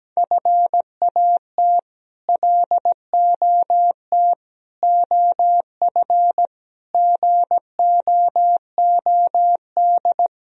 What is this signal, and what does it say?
Morse code, heard as a keyed tone, about 17 wpm, 700 Hz. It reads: FAT LOT OF GOOD
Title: Sentences from Top 1000 Words - Review Part 4 - 17wpm